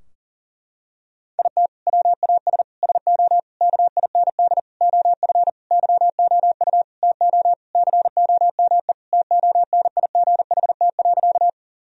Morse code, heard as a keyed tone, about 40 wpm, 700 Hz. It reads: IT WAS SO KIND OF YOU TO COME TONIGHT.
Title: Sentences from Top 1000 Words - Review Part 3 - 40wpm